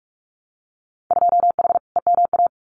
Morse code, heard as a keyed tone, about 46 wpm, 700 Hz. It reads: 2H EMA
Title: Winter Field Day Exchanges 45wpm